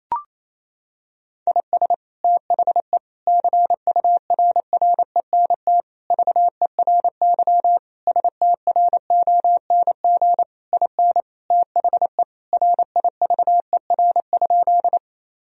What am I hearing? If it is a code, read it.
Morse code, heard as a keyed tone, about 28 wpm, 700 Hz. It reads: IS THE CURRENT VERY STRONG IN THE RIVER?
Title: Sentences from Top 1000 Words - Review Part 4 - 28wpm